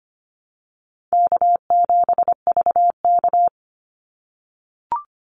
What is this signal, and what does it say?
Morse code, heard as a keyed tone, about 25 wpm, 700 Hz. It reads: K7VK
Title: NA SOTA Chaser Callsigns 25wpm